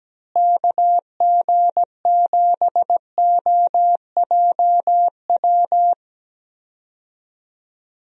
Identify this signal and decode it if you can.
Morse code, heard as a keyed tone, about 17 wpm, 700 Hz. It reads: KG7OJW
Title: USA Call Signs - 6 Characters - Repeated 2x - 17wpm